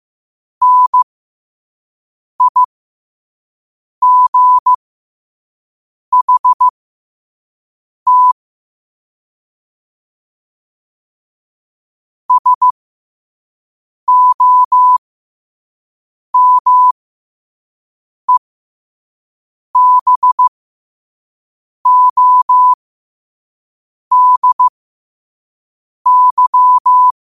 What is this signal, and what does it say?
Morse code, heard as a keyed tone, about 15 wpm, spaced as for 5 wpm, 1000 Hz. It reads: NIGHT SOMEBODY
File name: qotdfiles / 5wpm